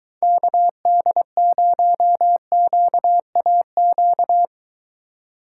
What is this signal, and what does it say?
Morse code, heard as a keyed tone, about 23 wpm, 700 Hz. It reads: KD0QAQ